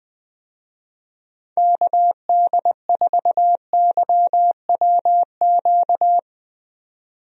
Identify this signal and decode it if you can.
Morse code, heard as a keyed tone, about 20 wpm, 700 Hz. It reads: KD4YWQ
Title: USA Call Signs - 6 Characters - 20wpm